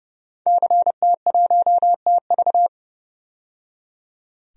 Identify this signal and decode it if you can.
Morse code, heard as a keyed tone, about 30 wpm, 700 Hz. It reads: CT1TV
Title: DXCC-100 Call Signs - 5 Characters - 30wpm